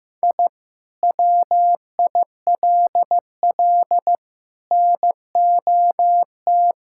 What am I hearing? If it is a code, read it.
Morse code, heard as a keyed tone, about 15 wpm, 700 Hz. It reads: I WILL NOT
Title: Sentences from Top 1000 Words - Review Part 6 - 15wpm